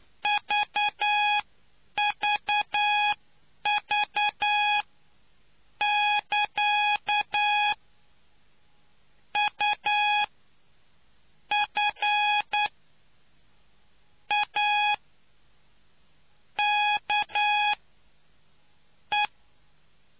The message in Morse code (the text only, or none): VVV <KA> UFAKE